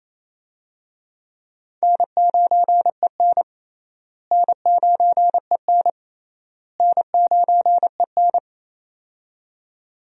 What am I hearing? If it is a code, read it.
Morse code, heard as a keyed tone, about 28 wpm, 700 Hz. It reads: N9EN N9EN N9EN
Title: USA Call Signs - 4 Characters - Repeated 3x - 28wpm